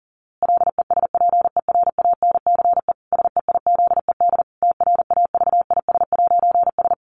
Morse code, heard as a keyed tone, about 40 wpm, 700 Hz. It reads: LESPERANCE SEIZED TRAVIS'S
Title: A Sound Of Thunder 40wpm